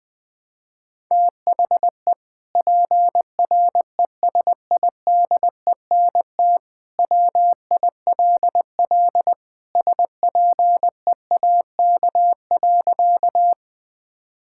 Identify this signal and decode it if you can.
Morse code, heard as a keyed tone, about 21 wpm, 700 Hz. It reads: THE PRESIDENT WILL SPEAK.
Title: Sentences with 2-Word Phrases - Spoken Beginning - 20wpm